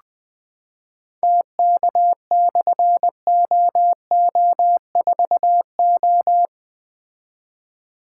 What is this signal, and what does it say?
Morse code, heard as a keyed tone, about 20 wpm, 700 Hz. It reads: TK/OO4O
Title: Difficult Call Signs 20wpm